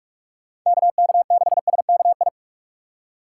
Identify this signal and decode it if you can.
Morse code, heard as a keyed tone, about 41 wpm, 700 Hz. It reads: KK6SKI